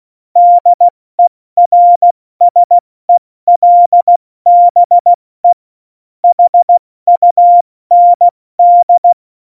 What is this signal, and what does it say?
Morse code, heard as a keyed tone, about 16 wpm, spaced as for 14 wpm, 700 Hz. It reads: DERSELBE HUND